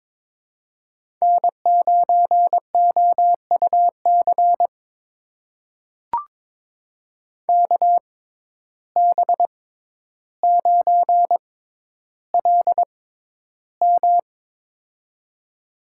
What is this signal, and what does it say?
Morse code, heard as a keyed tone, about 22 wpm, spaced as for 11 wpm, 700 Hz. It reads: N9OUC KB9LM
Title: USA Call Signs - 5 Characters - 6x Character Spacing - 22wpm